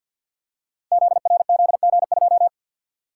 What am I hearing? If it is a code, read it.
Morse code, heard as a keyed tone, about 51 wpm, 700 Hz. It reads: ZRZG2
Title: Code Groups - 5 Letter:Numbers - 50wpm